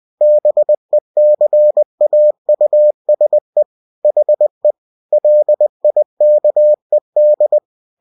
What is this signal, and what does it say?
Morse code, heard as a keyed tone, about 20 wpm, 600 Hz. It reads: BECAUSE HE LIKED
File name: cw-books / winnie-the-pooh / winnie-20wpm-page-0023